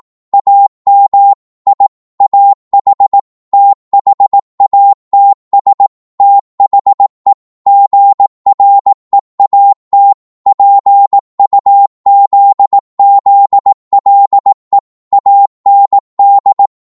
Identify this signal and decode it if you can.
Morse code, heard as a keyed tone, about 18 wpm, 800 Hz. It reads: AM I AH THATS THE GREAT PUZZLE AND
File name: Morse / psg-18wpm